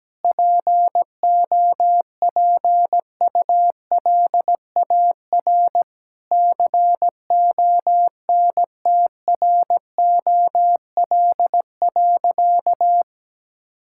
Morse code, heard as a keyed tone, about 17 wpm, 700 Hz. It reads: POPULAR CONTROL.
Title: Sentences from Top 1000 Words - Review Part 5 - 17wpm